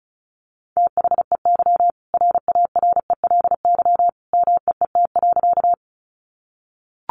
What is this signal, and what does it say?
Morse code, heard as a keyed tone, about 34 wpm, 700 Hz. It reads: THEY RARELY MEET.
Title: Sentences from Top 1400 Words 35wpm